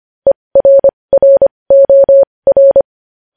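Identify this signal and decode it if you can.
Morse code, heard as a keyed tone, about 25 wpm, spaced as for 20 wpm, 550 Hz. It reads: ERROR
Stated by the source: Morse code practice of news headlines at 20 wpm.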